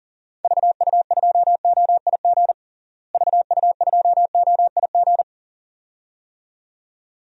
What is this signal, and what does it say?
Morse code, heard as a keyed tone, about 40 wpm, 700 Hz. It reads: VU2OIG VU2OIG
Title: DXCC-100 Call Signs - 6 Characters - Repeated 2x - 40wpm